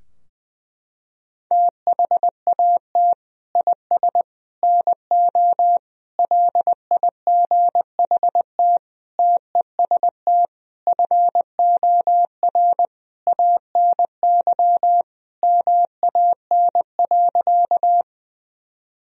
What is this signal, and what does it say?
Morse code, heard as a keyed tone, about 21 wpm, 700 Hz. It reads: THAT IS NO LIGHT TEST FOR ANY MAN.
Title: Sentences from Top 1000 Words - Review Part 3 - 20wpm